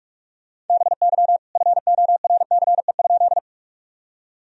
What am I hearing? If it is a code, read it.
Morse code, heard as a keyed tone, about 41 wpm, 700 Hz. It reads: BY FORCE?
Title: Sentences from Top 1000 Words - Review Part 3 - 45wpm